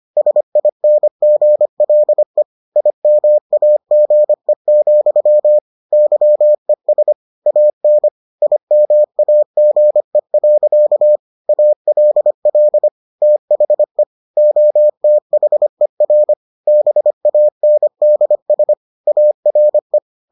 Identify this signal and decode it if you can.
Morse code, heard as a keyed tone, about 25 wpm, 600 Hz. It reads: SINGLE IMAGE, YES AN IMAGE. ALL THE OTHER BANDS ARE